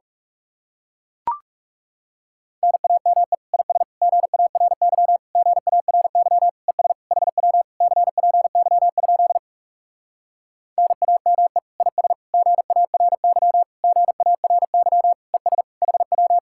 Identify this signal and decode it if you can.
Morse code, heard as a keyed tone, about 40 wpm, 700 Hz. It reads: NAME IS GARY GARY ES HW CPY? NAME IS GARY GARY ES HW